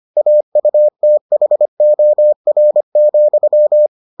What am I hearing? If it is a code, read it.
Morse code, heard as a keyed tone, about 25 wpm, 600 Hz. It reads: AUTHOR,